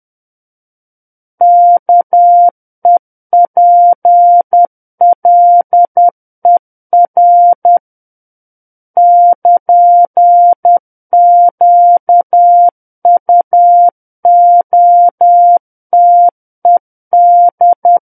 Morse code, heard as a keyed tone, about 10 wpm, 700 Hz. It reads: KEPLER <KN> QUOTED
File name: Book One / 01-Chapter 1 - 10 WPM